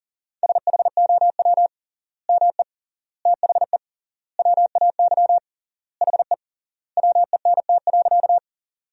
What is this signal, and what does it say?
Morse code, heard as a keyed tone, about 37 wpm, 700 Hz. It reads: SHOW ME THE WAY HE WENT.
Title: 3x Word-Spacing - Sentences from Top 200 Words - 40wpm